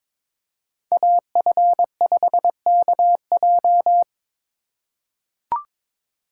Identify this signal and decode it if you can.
Morse code, heard as a keyed tone, about 21 wpm, 700 Hz. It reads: AF5KJ
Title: USA Call Signs - 3x Character Spacing - 22wpm